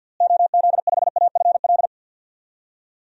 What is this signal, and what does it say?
Morse code, heard as a keyed tone, about 51 wpm, 700 Hz. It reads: OZ5AFL
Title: DXCC-100 Call Signs - 6 Characters - Repeated 3x - 50wpm